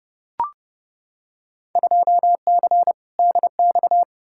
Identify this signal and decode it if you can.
Morse code, heard as a keyed tone, about 30 wpm, 700 Hz. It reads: 2C DX
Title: Field Day Exchanges Encore - 30wpm